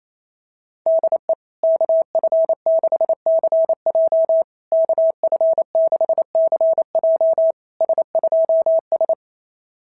Morse code, heard as a keyed tone, about 28 wpm, 660 Hz. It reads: DE KF6CJ KF6CJ S2S
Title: SOTA QSO Practice 28wpm